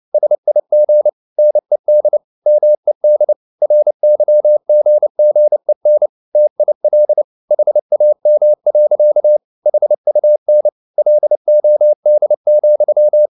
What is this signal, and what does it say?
Morse code, heard as a keyed tone, about 29 wpm, 600 Hz. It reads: SIG NED MED RYGGEN TIL HAM. HUN LOD,
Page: CW med Gnister